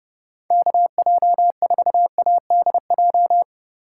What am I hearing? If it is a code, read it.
Morse code, heard as a keyed tone, about 30 wpm, 700 Hz. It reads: KJ4ADJ
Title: USA Call Signs - 6 Characters - 8x Character Spacing - 30wpm